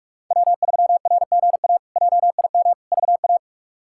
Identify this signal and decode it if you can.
Morse code, heard as a keyed tone, about 46 wpm, 700 Hz. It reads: W3RGA JIM VA
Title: K1USN SST QSO Practice - 45wpm